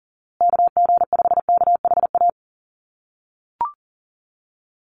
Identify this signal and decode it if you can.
Morse code, heard as a keyed tone, about 38 wpm, 700 Hz. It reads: KG5KHA
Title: USA Call Signs Repeated 3x 40wpm